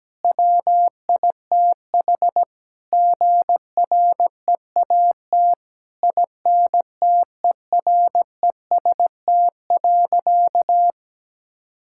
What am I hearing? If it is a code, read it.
Morse code, heard as a keyed tone, about 17 wpm, 700 Hz. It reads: WITH GREAT INTEREST.
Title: Sentences from Top 1000 Words - Review Part 6 - 17wpm